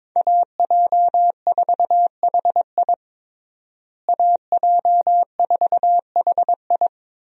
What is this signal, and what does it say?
Morse code, heard as a keyed tone, about 22 wpm, 700 Hz. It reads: AJ4HI AJ4HI
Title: USA Call Signs - 5 Characters - Repeated 3x - 22wpm